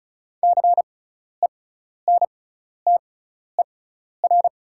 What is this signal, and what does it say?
Morse code, heard as a keyed tone, about 36 wpm, spaced as for 12 wpm, 690 Hz. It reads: CENTER
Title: 6x Character Spacing - Top 500 Words - 35wpm